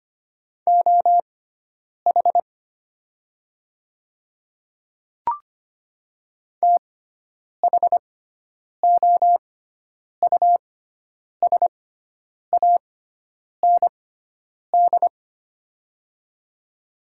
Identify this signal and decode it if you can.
Morse code, heard as a keyed tone, about 25 wpm, spaced as for 9 wpm, 700 Hz. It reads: OH THOUSAND
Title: 6x Character Spacing - Top 500 Words - 25wpm